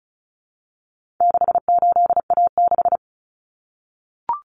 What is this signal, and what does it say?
Morse code, heard as a keyed tone, about 34 wpm, 700 Hz. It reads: 68A6
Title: Common Cut Numbers - 4-Digit 35wpm